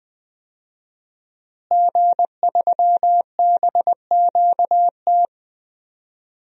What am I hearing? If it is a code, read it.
Morse code, heard as a keyed tone, about 19 wpm, 700 Hz. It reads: G3BQT
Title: DXCC-100 Call Signs - 5 Characters - Repeated 3x - 20wpm